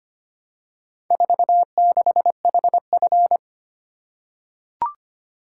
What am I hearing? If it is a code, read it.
Morse code, heard as a keyed tone, about 25 wpm, 700 Hz. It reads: 46HF
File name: practice / 25wpm / Four Letter-Number Combinations - 25wpm